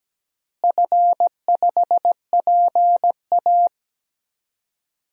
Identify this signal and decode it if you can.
Morse code, heard as a keyed tone, about 17 wpm, 700 Hz. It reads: F5PA
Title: DXCC-100 Call Signs - 4 Characters - Repeated 2x - 17wpm